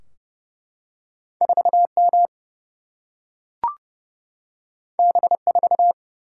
Morse code, heard as a keyed tone, about 29 wpm, 700 Hz. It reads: 4M B4
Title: Single-Letter Single-Number 30wpm